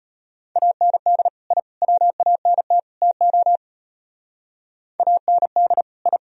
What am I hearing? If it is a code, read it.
Morse code, heard as a keyed tone, about 34 wpm, 700 Hz. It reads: AND I WANT TO AND I